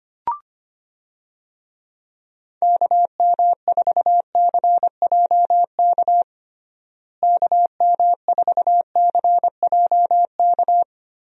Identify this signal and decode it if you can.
Morse code, heard as a keyed tone, about 25 wpm, 700 Hz. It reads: KM4CJK KM4CJK
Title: USA Call Signs - 6 Characters - Repeated 3x - 25wpm